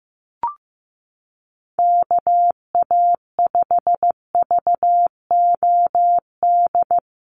Morse code, heard as a keyed tone, about 15 wpm, 700 Hz. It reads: KA5VOD
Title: USA Call Signs 15wpm